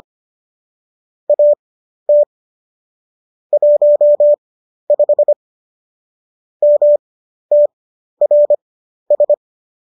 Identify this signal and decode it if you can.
Morse code, heard as a keyed tone, about 25 wpm, spaced as for 12 wpm, 590 Hz. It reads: AT 15 MTRS
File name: qso012 12